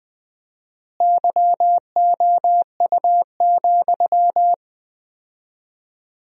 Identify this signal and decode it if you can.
Morse code, heard as a keyed tone, about 20 wpm, 700 Hz. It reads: YOU,
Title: 4x Word-Spacing - Sentences from Top 400 Words - 20wpm